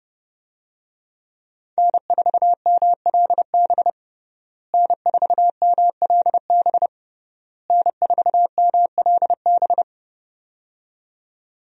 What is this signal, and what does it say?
Morse code, heard as a keyed tone, about 31 wpm, 700 Hz. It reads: N4MLB N4MLB N4MLB
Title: USA Call Signs - 5 Characters - Repeated 3x - 30wpm